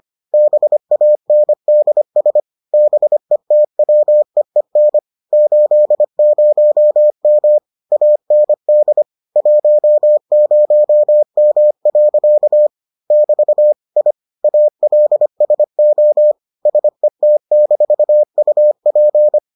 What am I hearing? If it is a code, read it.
Morse code, heard as a keyed tone, about 25 wpm, 600 Hz. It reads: BANDS BETWEEN 80M AND 10M. <BT> I ALSO SET-UP